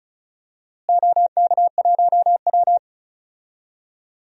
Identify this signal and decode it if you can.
Morse code, heard as a keyed tone, about 36 wpm, 700 Hz. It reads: OK1W
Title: DXCC-100 Call Signs - 4 Characters - Repeated 2x - 35wpm